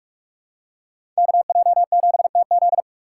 Morse code, heard as a keyed tone, about 40 wpm, 700 Hz. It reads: KJ7TZ